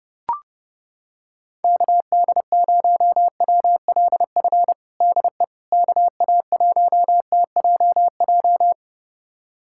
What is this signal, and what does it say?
Morse code, heard as a keyed tone, about 30 wpm, 700 Hz. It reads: KD0WLF DE KA1TJJ
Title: USA Callsign DE Callsign - 6 Characters - 30wpm